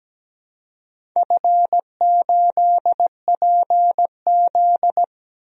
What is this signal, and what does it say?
Morse code, heard as a keyed tone, about 17 wpm, 700 Hz. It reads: F8PZ
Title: DXCC-100 Call Signs - 4 Characters - 17wpm